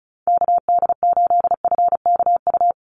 Morse code, heard as a keyed tone, about 36 wpm, 700 Hz. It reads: KD8FKU
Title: USA Call Signs 35wpm